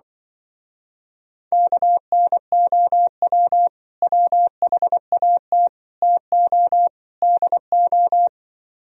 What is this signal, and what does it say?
Morse code, heard as a keyed tone, about 25 wpm, 700 Hz. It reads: KNOW WHAT TO DO